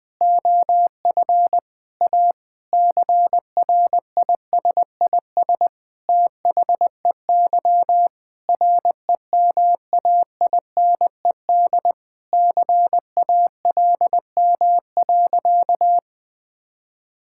Morse code, heard as a keyed tone, about 21 wpm, 700 Hz. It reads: OF A CRISIS THEY REMAINED CALM.
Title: Sentences with 5-Word Phrases - 20wpm